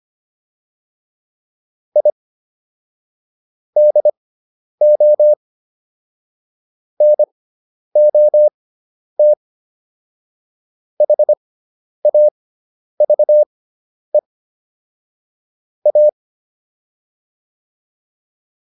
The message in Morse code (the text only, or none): I DO NOT HAVE A